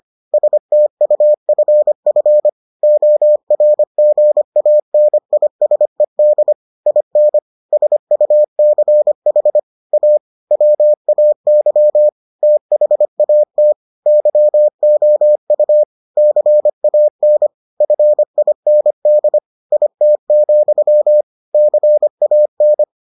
Morse code, heard as a keyed tone, about 25 wpm, 600 Hz. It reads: STUFF ORGANISED IN SUCH A WAY THAT YOU CAN FIND IT, CAN